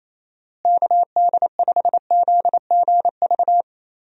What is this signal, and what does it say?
Morse code, heard as a keyed tone, about 27 wpm, 700 Hz. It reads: KD5ZGV
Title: USA Call Signs - 6 Characters - Repeated 2x - 28wpm